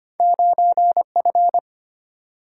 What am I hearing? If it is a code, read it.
Morse code, heard as a keyed tone, about 25 wpm, 700 Hz. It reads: 9F
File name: Two Letter-Number Combinations - 25wpm